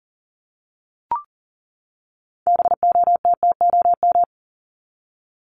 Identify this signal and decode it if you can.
Morse code, heard as a keyed tone, about 40 wpm, 700 Hz. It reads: BOTTOM
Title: Top 751-1000 Words 40wpm